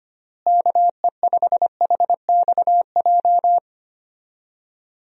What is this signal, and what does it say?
Morse code, heard as a keyed tone, about 25 wpm, 700 Hz. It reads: KE5HXJ
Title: USA Call Signs - 6 Characters - Repeated 2x - 25wpm